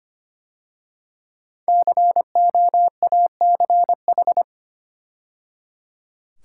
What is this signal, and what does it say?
Morse code, heard as a keyed tone, about 25 wpm, 700 Hz. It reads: COACH
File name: 5 Letter Words - First 3 Letters Given - 25wpm